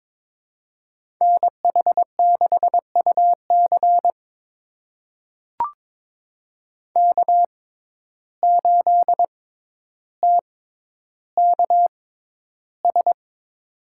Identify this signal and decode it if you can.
Morse code, heard as a keyed tone, about 22 wpm, spaced as for 11 wpm, 700 Hz. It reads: NH6UC K8TKS
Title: USA Call Signs - 6x Character Spacing - 22wpm